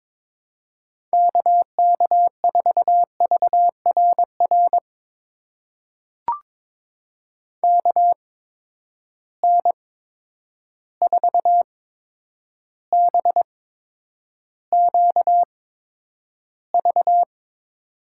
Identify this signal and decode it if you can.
Morse code, heard as a keyed tone, about 22 wpm, spaced as for 9 wpm, 700 Hz. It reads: KK4VRR KN4BQV
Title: USA Call Signs - 6 Characters - 8x Character Spacing - 22wpm